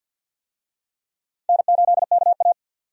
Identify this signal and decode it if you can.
Morse code, heard as a keyed tone, about 50 wpm, 700 Hz. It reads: N8KA
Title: USA Call Signs - 4 Characters - 50wpm